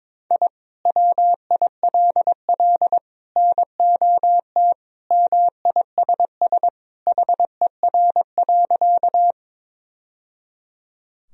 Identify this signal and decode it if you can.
Morse code, heard as a keyed tone, about 22 wpm, 700 Hz. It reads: I WILL NOT MISS HER.
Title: Sentences from Top 1000 Words - Review Part 3 - 22wpm